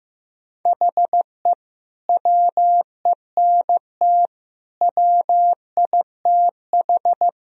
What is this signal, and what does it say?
Morse code, heard as a keyed tone, about 15 wpm, 700 Hz. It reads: HE WENT WITH
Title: Sentences from Top 1000 Words - Review Part 3 - 15wpm